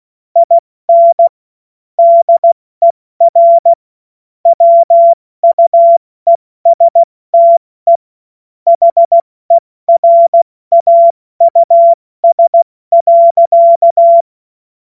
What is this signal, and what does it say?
Morse code, heard as a keyed tone, about 16 wpm, 680 Hz. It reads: IN DER WUESTE HERAUS.
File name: Morsekurs / Witze / Witz 28.November 2023 70 F80